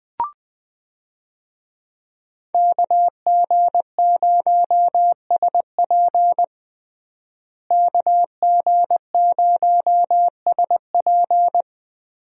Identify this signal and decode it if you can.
Morse code, heard as a keyed tone, about 21 wpm, 700 Hz. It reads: KG0SP KG0SP
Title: USA Call Signs - 5 Characters - Repeated 3x - 20wpm